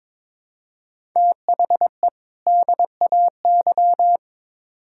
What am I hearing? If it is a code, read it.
Morse code, heard as a keyed tone, about 22 wpm, 700 Hz. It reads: THE DAY
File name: Build A Phrase - 2-Word Phrases - 22wpm